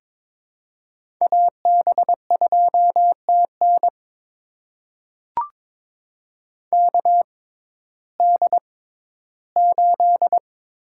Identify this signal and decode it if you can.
Morse code, heard as a keyed tone, about 22 wpm, spaced as for 13 wpm, 700 Hz. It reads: AB2TN KD8
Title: USA Call Signs - 6x Character Spacing - 22wpm